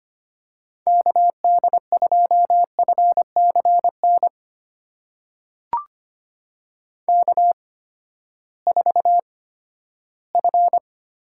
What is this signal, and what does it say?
Morse code, heard as a keyed tone, about 25 wpm, spaced as for 14 wpm, 700 Hz. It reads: KD2FCN K4F
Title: USA Call Signs - 8x Character Spacing - 25wpm